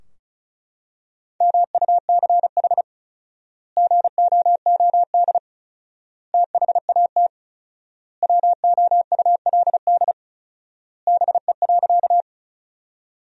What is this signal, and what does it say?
Morse code, heard as a keyed tone, about 36 wpm, 700 Hz. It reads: MUCH GOOD THAT WOULD BE.
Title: 4x Word-Spacing - Sentences from Top 200 Words - 35wpm